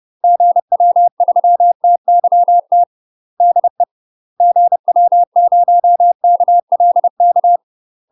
Morse code, heard as a keyed tone, about 30 wpm, 700 Hz. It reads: GW3TYT DE GW0KLK
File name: UK MASTER QSOs TRAINING / 30WPM / UK MASTER QSOs 0016